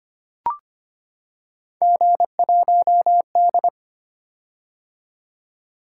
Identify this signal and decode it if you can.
Morse code, heard as a keyed tone, about 25 wpm, 700 Hz. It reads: G1D
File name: Three Letter-Number Combinations - 25wpm